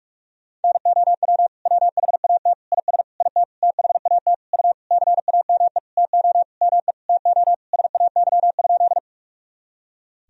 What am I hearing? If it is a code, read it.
Morse code, heard as a keyed tone, about 46 wpm, 700 Hz. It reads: NOW WHAT IS IT THAT U CAME TO ME TO SAY?
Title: Sentences from Top 200 Words and Abbreviations 45wpm